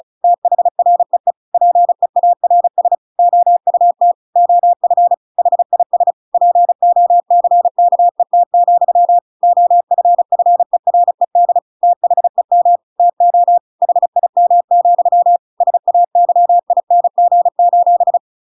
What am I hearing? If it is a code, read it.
Morse code, heard as a keyed tone, about 34 wpm, 700 Hz. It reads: THREE PEARS OUT OF HIS POCKET, OFFERED THEM TO HIM, SAYING:
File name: Pinocchio / Pinocchio ch7 35 35 ik1hge